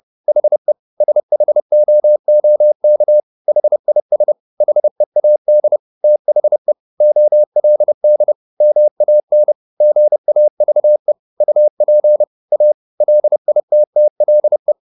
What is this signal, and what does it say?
Morse code, heard as a keyed tone, about 30 wpm, 600 Hz. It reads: HE SHOOK HIS HEAD THE OLD MAN GAVE UP A LITTLE